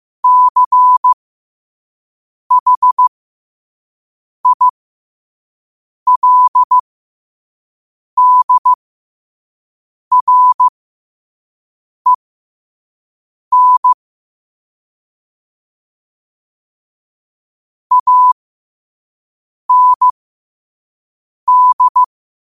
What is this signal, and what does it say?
Morse code, heard as a keyed tone, about 15 wpm, spaced as for 5 wpm, 1000 Hz. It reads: CHILDREN AND
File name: qotdfiles / 5wpm